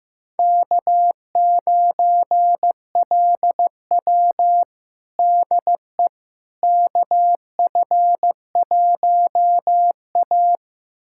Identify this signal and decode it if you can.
Morse code, heard as a keyed tone, about 15 wpm, 700 Hz. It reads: K9LW DE KF1A
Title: USA Callsign DE Callsign - 4 Characters - 15wpm